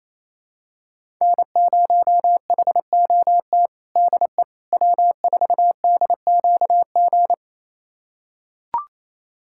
Morse code, heard as a keyed tone, about 28 wpm, 700 Hz. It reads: N0HOT DE W4DQG
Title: USA Callsign DE Callsign - 5 Characters - 28wpm